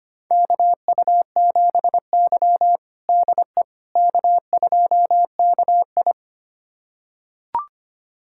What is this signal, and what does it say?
Morse code, heard as a keyed tone, about 25 wpm, 700 Hz. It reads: KU7Y DE K2KI
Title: USA Callsign DE Callsign - 4 Characters - 25wpm